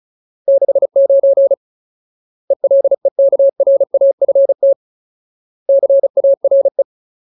Morse code, heard as a keyed tone, about 35 wpm, 550 Hz. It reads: 69 ELEKRAFT CARE
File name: Ultimate Word List / 35WPM Ultimate